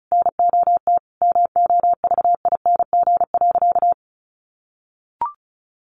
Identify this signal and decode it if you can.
Morse code, heard as a keyed tone, about 34 wpm, 700 Hz. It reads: NOT MOVING.
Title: A Sound Of Thunder 35wpm